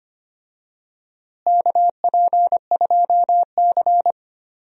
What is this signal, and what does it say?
Morse code, heard as a keyed tone, about 25 wpm, 700 Hz. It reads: KP2C
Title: DXCC-100 Call Signs - 4 Characters - Repeated 3x - 25wpm